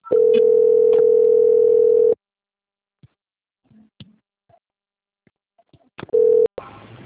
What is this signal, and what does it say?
Ringback tone (North America), 440 + 480 Hz.
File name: out- - - - - .14809